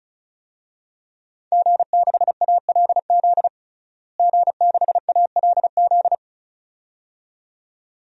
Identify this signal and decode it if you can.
Morse code, heard as a keyed tone, about 36 wpm, 700 Hz. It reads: G6ALZ G6ALZ
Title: DXCC-100 Call Signs - 5 Characters - Repeated 2x - 35wpm